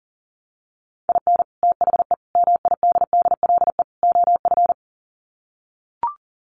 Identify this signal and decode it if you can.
Morse code, heard as a keyed tone, about 40 wpm, 700 Hz. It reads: IN THE MIDDLE OF
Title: Sets of 4 Words 40wpm